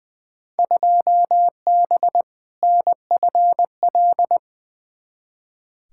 Morse code, heard as a keyed tone, about 20 wpm, 700 Hz. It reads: 2B NFL
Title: Field Day Exchanges Encore - 20wpm